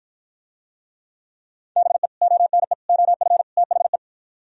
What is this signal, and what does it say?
Morse code, heard as a keyed tone, about 52 wpm, 700 Hz. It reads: BE ONE OF THE